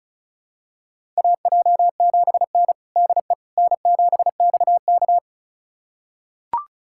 Morse code, heard as a keyed tone, about 34 wpm, 700 Hz. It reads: AJ7N DE N7XK
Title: USA Callsign DE Callsign - 4 Characters - 35wpm